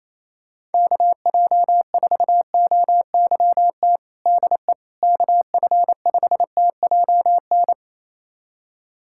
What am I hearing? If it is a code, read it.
Morse code, heard as a keyed tone, about 28 wpm, 700 Hz. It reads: KJ4OYT DE KF5TJN
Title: USA Callsign DE Callsign - 6 Characters - 28wpm